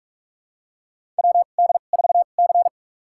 Morse code, heard as a keyed tone, about 48 wpm, spaced as for 33 wpm, 700 Hz. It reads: WD4C
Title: USA Call Signs - 4 Characters - 2x Character Spacing - 45wpm